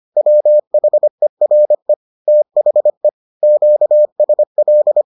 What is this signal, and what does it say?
Morse code, heard as a keyed tone, about 25 wpm, 600 Hz. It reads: WHERE THE QSL